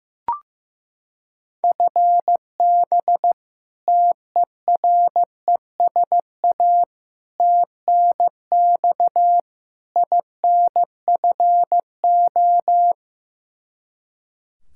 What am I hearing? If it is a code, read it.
Morse code, heard as a keyed tone, about 15 wpm, 700 Hz. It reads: FB TERESA TNX INFO
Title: Practice QSO - Exchange 3 - More Info - 15wpm